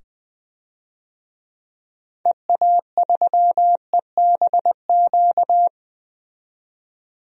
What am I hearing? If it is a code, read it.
Morse code, heard as a keyed tone, about 20 wpm, 700 Hz. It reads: EA3EBQ
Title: DXCC-100 Call Signs - 6 Characters - 20wpm